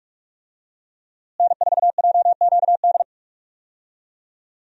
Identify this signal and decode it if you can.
Morse code, heard as a keyed tone, about 42 wpm, 700 Hz. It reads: N4JQD
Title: USA Call Signs - 5 Characters - Repeated 3x - 45wpm